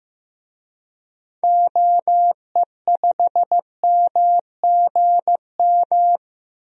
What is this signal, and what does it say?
Morse code, heard as a keyed tone, about 15 wpm, 700 Hz. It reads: OE5MGM
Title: DXCC-100 Call Signs - 6 Characters - 15wpm